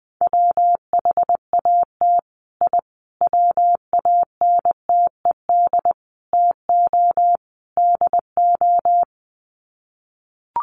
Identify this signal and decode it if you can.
Morse code, heard as a keyed tone, about 20 wpm, 700 Hz. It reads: WHAT I WANTED TO DO
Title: Sets of 5 Words Encore2 20wpm